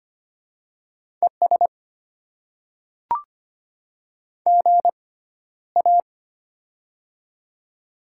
Morse code, heard as a keyed tone, about 25 wpm, spaced as for 13 wpm, 700 Hz. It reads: ES GA
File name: Top 20 QSO Elements - 6x Character Spacing - 25wpm